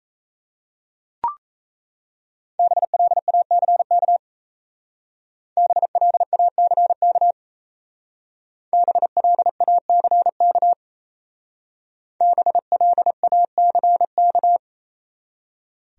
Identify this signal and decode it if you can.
Morse code, heard as a keyed tone, about 33 wpm, 700 Hz. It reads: BLACK BLACK BLACK BLACK
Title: Speed-Racing Top 500 Words 28wpm